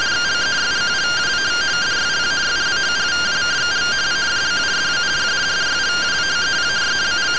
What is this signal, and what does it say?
Two-tone teleprinter signal, about 170 Hz shift.